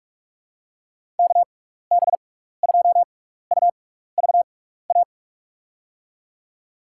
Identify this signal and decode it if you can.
Morse code, heard as a keyed tone, about 46 wpm, spaced as for 15 wpm, 700 Hz. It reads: KB2UVA
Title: USA Call Signs - 6 Characters - 6x Character Spacing - 45wpm